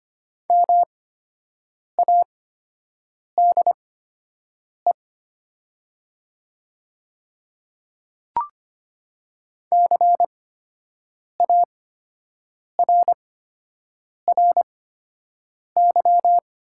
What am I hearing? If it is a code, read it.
Morse code, heard as a keyed tone, about 25 wpm, spaced as for 7 wpm, 700 Hz. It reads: MADE CARRY
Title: 8x Character Spacing - Top 300 Words - 25wpm